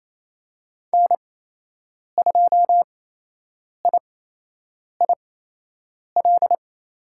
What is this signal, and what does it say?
Morse code, heard as a keyed tone, about 27 wpm, spaced as for 8 wpm, 700 Hz. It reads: N2IIL
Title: USA Call Signs - 5 Characters - 8x Character Spacing - 28wpm